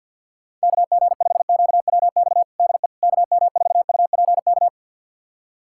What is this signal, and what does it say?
Morse code, heard as a keyed tone, about 48 wpm, 700 Hz. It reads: KG5QWX DE KM4UPK
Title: USA Callsign DE Callsign - 6 Characters - 50wpm